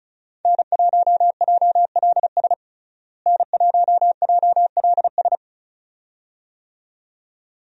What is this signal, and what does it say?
Morse code, heard as a keyed tone, about 36 wpm, 700 Hz. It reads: N1JLS N1JLS
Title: USA Call Signs - 5 Characters - Repeated 2x - 35wpm